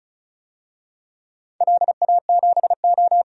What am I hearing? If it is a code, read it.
Morse code, heard as a keyed tone, about 36 wpm, 700 Hz. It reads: LA7O
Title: DXCC-100 Call Signs - 4 Characters - 35wpm